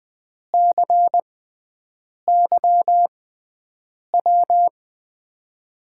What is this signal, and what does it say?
Morse code, heard as a keyed tone, about 20 wpm, spaced as for 7 wpm, 700 Hz. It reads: CYW
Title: USA Call Signs - 5 Characters - 6x Character Spacing - 20wpm